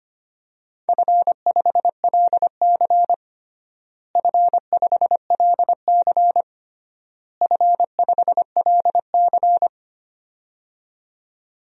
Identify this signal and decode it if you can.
Morse code, heard as a keyed tone, about 25 wpm, 700 Hz. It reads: F5LC F5LC F5LC
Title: DXCC-100 Call Signs - 4 Characters - Repeated 3x - 25wpm